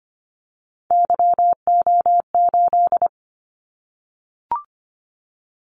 Morse code, heard as a keyed tone, about 25 wpm, 700 Hz. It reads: YO8
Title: Callsign Prefixes for Top 25 DXCC - 25wpm